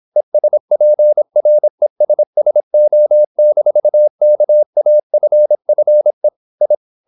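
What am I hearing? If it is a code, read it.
Morse code, heard as a keyed tone, about 26 wpm, 600 Hz. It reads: ESPRESSO-KAFFE I
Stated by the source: High-speed (QRQ) Morse at 26 wpm.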